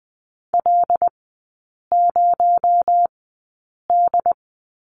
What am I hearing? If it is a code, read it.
Morse code, heard as a keyed tone, about 20 wpm, spaced as for 8 wpm, 700 Hz. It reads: L0D
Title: 218 Lesson 35e - Number 0 - Three Characters 20wpm